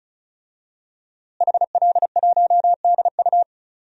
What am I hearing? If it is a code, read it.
Morse code, heard as a keyed tone, about 36 wpm, 700 Hz. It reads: HL1DU
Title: DXCC-100 Call Signs - 5 Characters - Repeated 2x - 35wpm